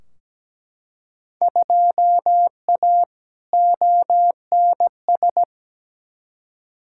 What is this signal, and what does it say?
Morse code, heard as a keyed tone, about 17 wpm, 700 Hz. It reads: 2A ONS
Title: Field Day Exchanges Encore - 17wpm